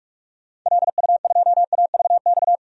Morse code, heard as a keyed tone, about 46 wpm, 690 Hz. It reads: LU2AVX
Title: DXCC-100 Call Signs - 6 Characters - 45wpm